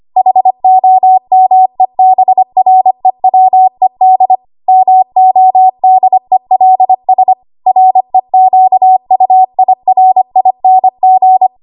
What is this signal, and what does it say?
Morse code, heard as a keyed tone, about 25 wpm, 750 Hz. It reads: HOMEBREWED MODELS REQUIRING
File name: Archive / 25 WPM / 25WPM